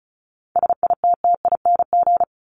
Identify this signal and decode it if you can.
Morse code, heard as a keyed tone, about 34 wpm, 700 Hz. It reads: SITTING
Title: ING Suffix - Top 100 Words with Base Words 35wpm